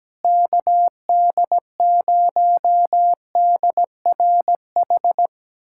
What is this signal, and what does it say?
Morse code, heard as a keyed tone, about 17 wpm, 700 Hz. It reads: KD0DRH
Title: USA Callsign DE Callsign - 6 Characters - 17wpm